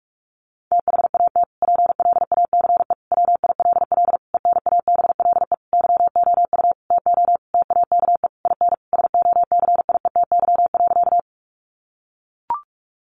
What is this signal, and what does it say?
Morse code, heard as a keyed tone, about 44 wpm, 700 Hz. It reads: THAT PLACE WILL ENABLE YOU TO TAKE IN SOCIETY.
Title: Sentences from Top 1500 Words 45wpm